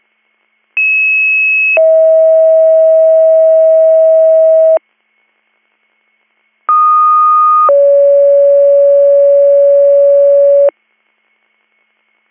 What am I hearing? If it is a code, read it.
Two-tone page, heard as a one-second tone, then a three-second tone, about 2600 then 640 Hz, about 1230 then 560 Hz. It is heard from 0.8 to 10.7 s.